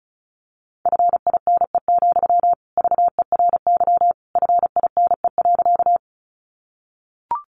Morse code, heard as a keyed tone, about 34 wpm, 700 Hz. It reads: FINE, VERY FINE.
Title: Sentences from Top 500 Words 35wpm